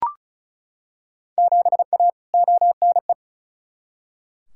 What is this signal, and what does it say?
Morse code, heard as a keyed tone, about 36 wpm, 690 Hz. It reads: 7A ONE
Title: Field Day Exchanges Encore - 35wpm